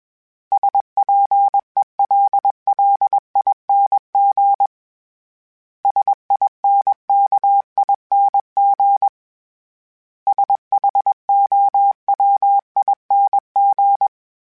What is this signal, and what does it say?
Morse code, heard as a keyed tone, about 21 wpm, 800 Hz. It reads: SPELLING SINKING SHOWING